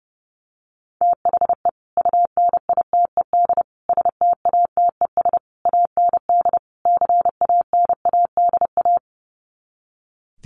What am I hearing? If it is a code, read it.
Morse code, heard as a keyed tone, about 30 wpm, 700 Hz. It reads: THE UNITED STATES AND CANADA
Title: Sets of 5 Words 30wpm